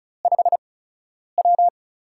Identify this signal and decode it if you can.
Morse code, heard as a keyed tone, about 36 wpm, spaced as for 10 wpm, 700 Hz. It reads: 5W
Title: USA Call Signs - 4 Characters - 8x Character Spacing - 35wpm